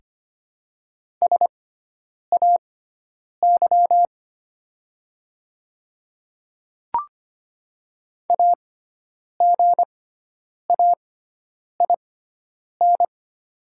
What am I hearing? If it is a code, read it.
Morse code, heard as a keyed tone, about 25 wpm, spaced as for 9 wpm, 700 Hz. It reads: SAY AGAIN
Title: 6x Character Spacing - Top 200 Words - 25wpm